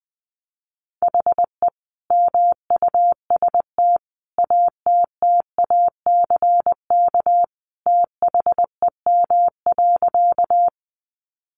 Morse code, heard as a keyed tone, about 20 wpm, 700 Hz. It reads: HE MUST ATTACK THEM.
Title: Sentences from Top 1500 Words 20wpm